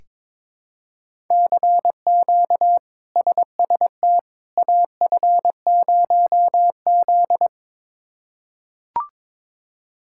Morse code, heard as a keyed tone, about 22 wpm, 700 Hz. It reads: CQ SST AF0Z
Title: K1USN SST QSO Practice - 22wpm